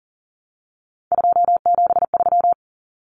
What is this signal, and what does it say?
Morse code, heard as a keyed tone, about 40 wpm, 700 Hz. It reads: 273
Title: Numbers - 3-Digit 40wpm